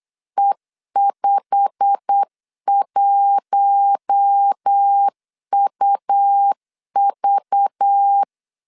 Morse code, heard as a keyed tone, about 8 wpm, 800 Hz. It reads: E51UV
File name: LEZIONI / 8 YJ45 / 8 YJ45 8wpm 3214